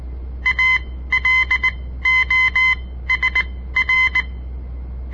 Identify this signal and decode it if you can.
Morse code, heard as a keyed tone, about 19 wpm, spaced as for 14 wpm, 2050 Hz. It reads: ALOSR
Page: On the air Morse code practice from the Southern Montana Amatuer Radio Association; all code, no read back